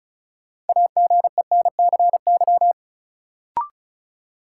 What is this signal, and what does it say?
Morse code, heard as a keyed tone, about 34 wpm, 700 Hz. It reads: AGENCY
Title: Sentences from Top 1300 Words - New Word Drill Alt - 35wpm